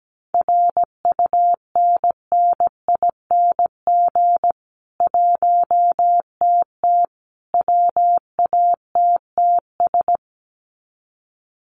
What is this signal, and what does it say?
Morse code, heard as a keyed tone, about 17 wpm, 700 Hz. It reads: RUNNING 1TT WATTS
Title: Common QSO Phrases 17wpm